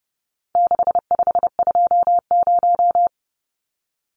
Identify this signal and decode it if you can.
Morse code, heard as a keyed tone, about 30 wpm, 700 Hz. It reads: 6520
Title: Numbers - 4-Digit 30wpm